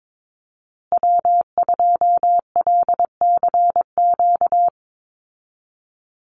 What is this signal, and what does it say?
Morse code, heard as a keyed tone, about 22 wpm, 700 Hz. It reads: W2LCQ
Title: Call Signs Repeated 2x - 22wpm